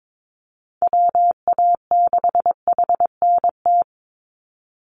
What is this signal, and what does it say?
Morse code, heard as a keyed tone, about 22 wpm, 700 Hz. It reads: WA6HNT
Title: USA Call Signs 22wpm